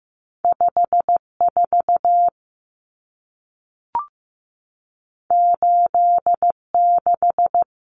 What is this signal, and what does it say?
Morse code, heard as a keyed tone, about 15 wpm, 700 Hz. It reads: 54 86
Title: Numbers - 2-Digit 15wpm